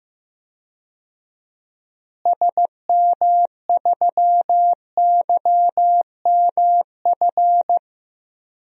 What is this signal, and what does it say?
Morse code, heard as a keyed tone, about 15 wpm, 700 Hz. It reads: SM3YMF
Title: DXCC-100 Call Signs - 6 Characters - Repeated 2x - 15wpm